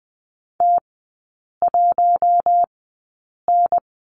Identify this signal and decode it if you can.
Morse code, heard as a keyed tone, about 20 wpm, spaced as for 8 wpm, 690 Hz. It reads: T1N
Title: 093 Lesson 16e - Letter c - Three Characters 20wpm